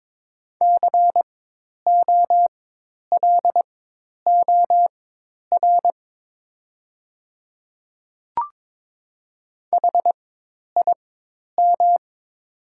Morse code, heard as a keyed tone, about 22 wpm, spaced as for 10 wpm, 700 Hz. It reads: COLOR HIM
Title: 4x Character Spacing - Top 500 Words - 22wpm